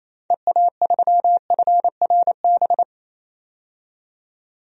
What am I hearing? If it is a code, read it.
Morse code, heard as a keyed tone, about 28 wpm, 700 Hz. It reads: EA3FRB
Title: DXCC-100 Call Signs - 6 Characters - 28wpm